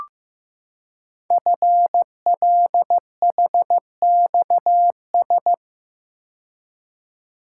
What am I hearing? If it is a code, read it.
Morse code, heard as a keyed tone, about 15 wpm, 700 Hz. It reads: FLHXS
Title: Code Groups - 5 Letters - 15wpm